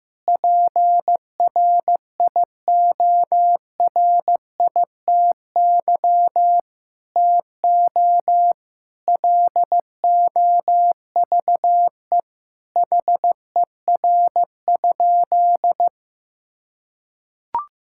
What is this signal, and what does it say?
Morse code, heard as a keyed tone, about 15 wpm, 700 Hz. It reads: PRIORITY TO LOVE HER?
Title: Sentences from Top 1600 Words 15wpm